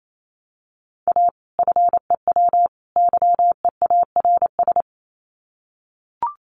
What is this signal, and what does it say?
Morse code, heard as a keyed tone, about 28 wpm, 700 Hz. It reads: A FEW YEARS
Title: Sets of 3 Words 28wpm